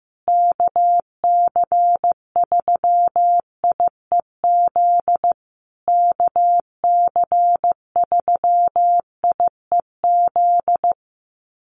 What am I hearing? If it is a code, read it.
Morse code, heard as a keyed tone, about 15 wpm, 700 Hz. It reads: KC3IEZ KC3IEZ
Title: USA Call Signs Repeated 3x 15wpm